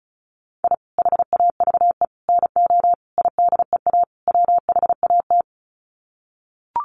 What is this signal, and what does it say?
Morse code, heard as a keyed tone, about 34 wpm, 700 Hz. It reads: I HAVE NO IDEA WHAT
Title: Sets of 5 Words Encore 35wpm